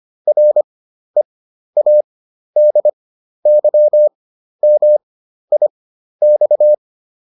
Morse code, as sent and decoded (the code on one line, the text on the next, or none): .-. .- -.. -.-- -- .. -..-
READYMIX